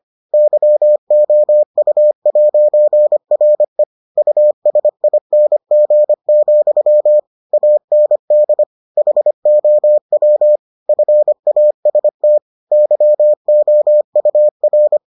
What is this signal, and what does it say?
Morse code, heard as a keyed tone, about 25 wpm, 600 Hz. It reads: YOU'RE USING, AND HOW FAST YOUR